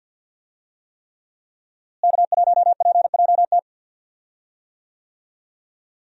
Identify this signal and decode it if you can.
Morse code, heard as a keyed tone, about 45 wpm, 700 Hz. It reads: K1PJT
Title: USA Call Signs - 5 Characters - Repeated 3x - 50wpm